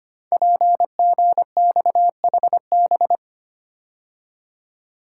Morse code, heard as a keyed tone, about 25 wpm, 700 Hz. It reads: PGXHB
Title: Code Groups - 5 Letters - 25wpm